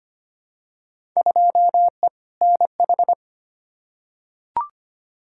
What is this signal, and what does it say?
Morse code, heard as a keyed tone, about 25 wpm, 700 Hz. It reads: 2E NH
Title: Field Day Exchanges - 25wpm